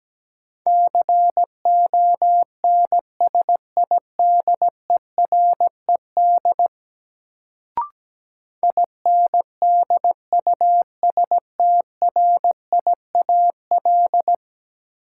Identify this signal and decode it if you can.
Morse code, heard as a keyed tone, about 17 wpm, 700 Hz. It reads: CONSIDERED INDUSTRIAL
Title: Sets of 5 Words - New Word Drill - 17wpm